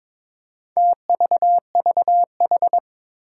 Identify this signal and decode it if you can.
Morse code, heard as a keyed tone, about 22 wpm, 700 Hz. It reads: TVVH
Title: Four Challenging Letter-Number Combinations 22wpm